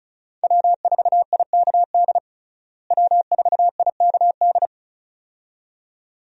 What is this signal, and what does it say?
Morse code, heard as a keyed tone, about 36 wpm, 700 Hz. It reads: W4IKD W4IKD
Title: USA Call Signs - 5 Characters - Repeated 2x - 35wpm